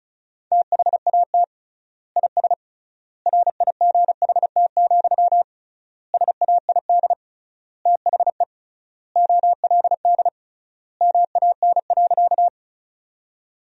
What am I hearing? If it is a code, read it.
Morse code, heard as a keyed tone, about 36 wpm, 700 Hz. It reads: THAT IS RIGHT, SAID THE OLD MAN.
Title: 3x Word-Spacing - Sentences from Top 200 Words - 35wpm